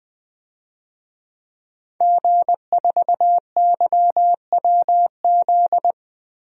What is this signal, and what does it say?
Morse code, heard as a keyed tone, about 21 wpm, 700 Hz. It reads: G4YWZ
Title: DXCC-100 Call Signs - 5 Characters - 20wpm